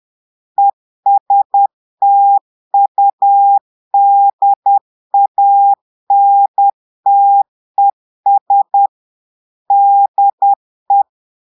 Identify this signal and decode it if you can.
Morse code, heard as a keyed tone, about 10 wpm, 800 Hz. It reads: ESTUDANTES DE
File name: Portugais - texte en clair 1 - vitesse - 10 mots minute